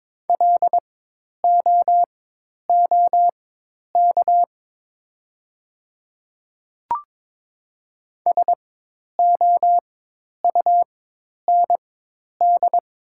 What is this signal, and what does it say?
Morse code, heard as a keyed tone, about 22 wpm, spaced as for 10 wpm, 700 Hz. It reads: LOOK SOUND
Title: 4x Character Spacing - Top 100 Words - 22wpm